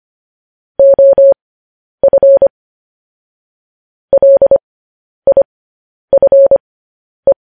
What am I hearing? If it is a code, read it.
Morse code, heard as a keyed tone, about 25 wpm, spaced as for 10 wpm, 550 Hz. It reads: OF LIFE
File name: podcast / News headlines-1x-25x10